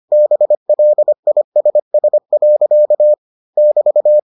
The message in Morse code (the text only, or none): BLISS. <BT>